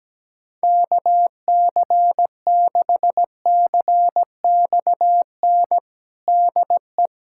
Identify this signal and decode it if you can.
Morse code, heard as a keyed tone, about 17 wpm, 700 Hz. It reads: KC6CXN DE
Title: USA Callsign DE Callsign - 6 Characters - 17wpm